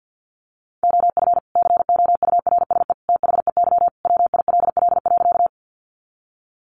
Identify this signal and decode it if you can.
Morse code, heard as a keyed tone, about 50 wpm, 700 Hz. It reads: OF COURSE THEY WILL.
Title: Sentences from Top 500 Words 50wpm